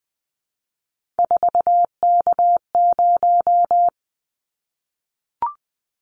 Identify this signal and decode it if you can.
Morse code, heard as a keyed tone, about 19 wpm, 700 Hz. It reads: 4K0
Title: Callsign Prefixes for Top 100 DXCC - 20wpm